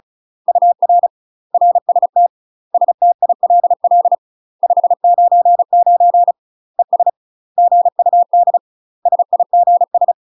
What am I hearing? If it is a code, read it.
Morse code, heard as a keyed tone, about 35 wpm, 700 Hz. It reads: UR RST STILL 599 ES GUD SIGS
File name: UK MASTER QSOs TRAINING / 35WPM / UK MASTER QSOs 0018